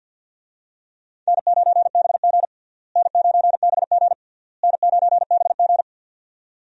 Morse code, heard as a keyed tone, about 50 wpm, 700 Hz. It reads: N9BG N9BG N9BG
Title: USA Call Signs - 4 Characters - Repeated 3x - 50wpm